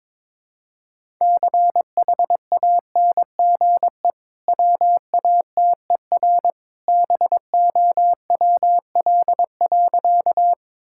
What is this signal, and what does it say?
Morse code, heard as a keyed tone, about 22 wpm, 690 Hz. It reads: CHANGE WATER BOWL.
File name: practice / 22wpm / Easy Sentences with Hints - 22wpm